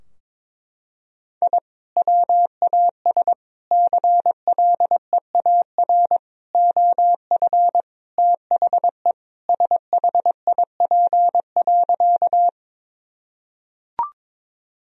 Morse code, heard as a keyed tone, about 22 wpm, 700 Hz. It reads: I WAS CLEAR OF THE SHIP.
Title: Sentences from Top 1000 Words - Review Part 2 - 22wpm